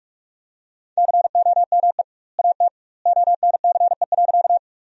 Morse code, heard as a keyed tone, about 46 wpm, 700 Hz. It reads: COME AT ONCE.
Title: 2x Word-Spacing - Sentences from Top 300 Words - 45wpm